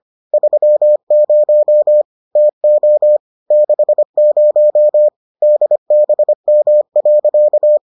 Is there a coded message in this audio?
Yes: Morse code from 0.3 to 7.8 s, heard as a keyed tone, about 25 wpm, 600 Hz. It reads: 30 TO 60 DBM.